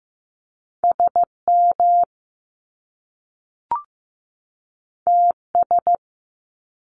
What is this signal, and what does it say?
Morse code, heard as a keyed tone, about 15 wpm, 700 Hz. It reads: SM TS
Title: Top Two Letter Combinations in Words 15wpm